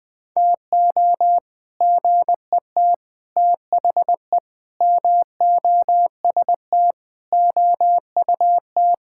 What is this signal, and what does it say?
Morse code, heard as a keyed tone, about 21 wpm, 700 Hz. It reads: TO GET THE MOST OUT
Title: Sentences with 7-Word Phrases - 20wpm